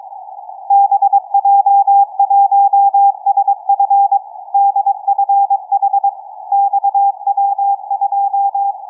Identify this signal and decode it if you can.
Morse code, heard as a keyed tone, about 21 wpm, 800 Hz. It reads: BJ1SF DFH XW2
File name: XW2E-CAS3E-VERT-01 18h28 utc cw